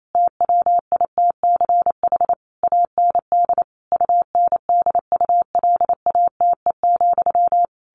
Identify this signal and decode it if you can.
Morse code, heard as a keyed tone, about 28 wpm, 700 Hz. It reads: TWITCH AND UNDULATE,
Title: A Sound Of Thunder 28wpm